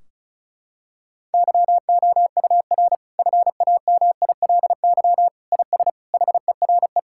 Morse code, heard as a keyed tone, about 34 wpm, 700 Hz. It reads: YOUR FAMILY IS HERE
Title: Sentences from Top 1600 Words 35wpm